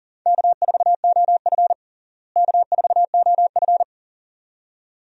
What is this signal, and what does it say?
Morse code, heard as a keyed tone, about 37 wpm, 700 Hz. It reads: K4OF K4OF
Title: USA Call Signs - 4 Characters - Repeated 2x - 40wpm